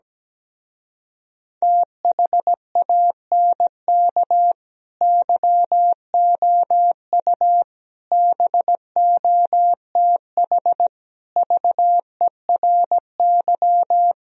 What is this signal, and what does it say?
Morse code, heard as a keyed tone, about 17 wpm, 700 Hz. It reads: THANK YOU BOTH VERY
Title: Speed-Racing - Sets of 5 Words 17wpm